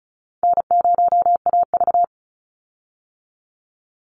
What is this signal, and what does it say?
Morse code, heard as a keyed tone, about 34 wpm, 700 Hz. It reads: N0AV
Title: Easy Call Signs Repeated 3x - 35wpm